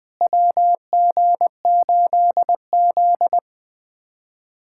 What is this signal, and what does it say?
Morse code, heard as a keyed tone, about 21 wpm, 700 Hz. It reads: WG8Z
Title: USA Callsign DE Callsign - 4 Characters - 20wpm